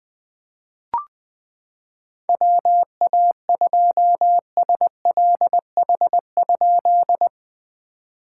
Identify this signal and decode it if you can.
Morse code, heard as a keyed tone, about 20 wpm, 690 Hz. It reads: WA2SLH?
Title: POTA QSO Practice - 20wpm